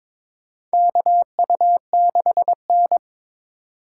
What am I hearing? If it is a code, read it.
Morse code, heard as a keyed tone, about 22 wpm, 700 Hz. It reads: KU6N
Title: USA Call Signs - 4 Characters - 2x Character Spacing - 22wpm